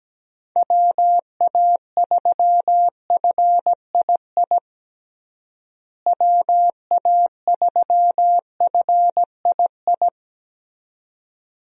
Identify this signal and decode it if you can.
Morse code, heard as a keyed tone, about 17 wpm, 700 Hz. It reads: WA3FII WA3FII
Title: USA Call Signs - 6 Characters - Repeated 2x - 17wpm